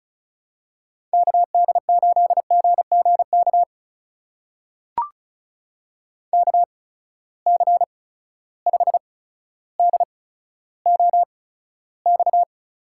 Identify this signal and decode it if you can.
Morse code, heard as a keyed tone, about 35 wpm, spaced as for 15 wpm, 700 Hz. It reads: KD8GGK KC5DOX
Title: USA Call Signs - 6 Characters - 8x Character Spacing - 35wpm